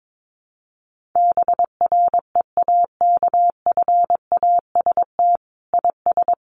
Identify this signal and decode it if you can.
Morse code, heard as a keyed tone, about 22 wpm, 700 Hz. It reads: BREAKFAST IS
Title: Sentences from Top 1500 Words 22wpm